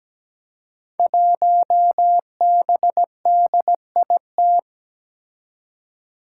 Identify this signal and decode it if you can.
Morse code, heard as a keyed tone, about 17 wpm, 700 Hz. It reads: 1BDIT
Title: Five Challenging Letter-Number Combinations 17wpm